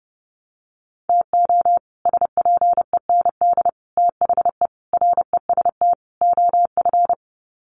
Morse code, heard as a keyed tone, about 30 wpm, 700 Hz. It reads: TO SPEND THE REST OF
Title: Sets of 5 Words Encore2 30wpm